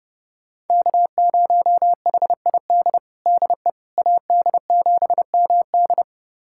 Morse code, heard as a keyed tone, about 30 wpm, 700 Hz. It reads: K0HID DE AD7MD
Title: USA Callsign DE Callsign - 5 Characters - 30wpm